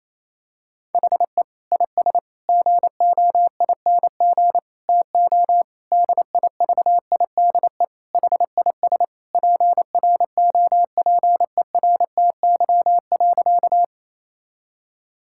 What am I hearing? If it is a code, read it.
Morse code, heard as a keyed tone, about 27 wpm, 700 Hz. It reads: HE IS GOING TO DIVIDE HIS PROPERTY.
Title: Sentences from Top 1600 Words 28wpm